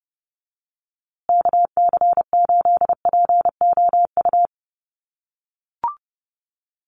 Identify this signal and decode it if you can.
Morse code, heard as a keyed tone, about 30 wpm, 700 Hz. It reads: KC8POU
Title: USA Call Signs Repeated 3x 30wpm